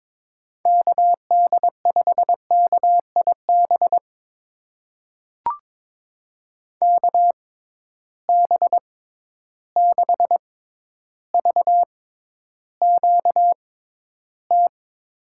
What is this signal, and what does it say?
Morse code, heard as a keyed tone, about 21 wpm, spaced as for 11 wpm, 700 Hz. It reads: KD5KIB KB6VQT
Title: USA Call Signs - 6 Characters - 6x Character Spacing - 22wpm